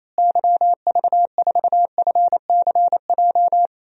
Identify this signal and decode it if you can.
Morse code, heard as a keyed tone, about 28 wpm, 700 Hz. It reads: YV4FCJ
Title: DXCC-100 Call Signs - 6 Characters - Repeated 3x - 28wpm